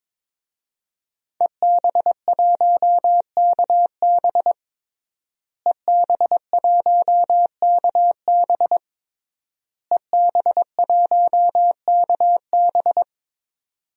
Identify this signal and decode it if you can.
Morse code, heard as a keyed tone, about 22 wpm, 700 Hz. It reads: EB1KB EB1KB EB1KB
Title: DXCC-100 Call Signs - 5 Characters - Repeated 3x - 22wpm